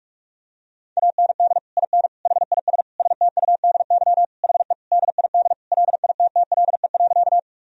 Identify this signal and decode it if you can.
Morse code, heard as a keyed tone, about 41 wpm, 700 Hz. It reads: AND IN HIS STUDY HE DID LITTLE.
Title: Sentences from Top 1000 Words - Review Part 1 - 45wpm